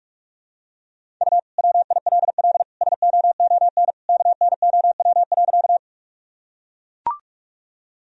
Morse code, heard as a keyed tone, about 44 wpm, 700 Hz. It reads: U WILL SOON KNOW.
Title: Sentences from Top 400 Words and Abbreviations 45wpm